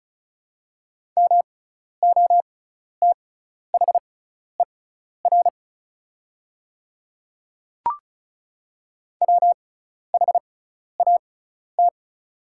Morse code, heard as a keyed tone, about 36 wpm, spaced as for 12 wpm, 690 Hz. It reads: MOTHER WHAT
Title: 6x Character Spacing - Top 200 Words - 35wpm